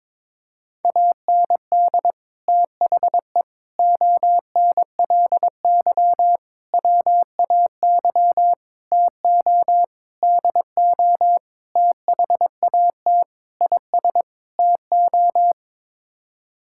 Morse code, heard as a keyed tone, about 22 wpm, 700 Hz. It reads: AND THE ONLY WAY TO DO THAT IS TO
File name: Sets of 9 Words - 22wpm